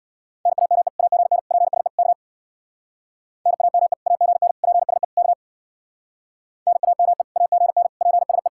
Morse code, heard as a keyed tone, about 64 wpm, 700 Hz. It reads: NAME AGN PSE K NAME AGN PSE K NAME AGN PSE